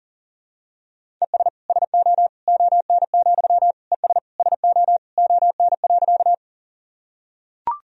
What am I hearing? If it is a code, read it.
Morse code, heard as a keyed tone, about 39 wpm, 700 Hz. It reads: ES SO ON, ES SO ON.
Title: Sentences from Top 100 Words and Abbreviations 40wpm